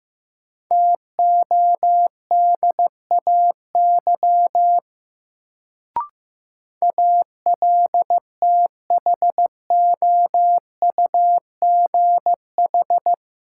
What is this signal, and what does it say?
Morse code, heard as a keyed tone, about 15 wpm, 700 Hz. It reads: TODAY ALTHOUGH
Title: Sentences from Top 600 Words - New Word Drill - 15wpm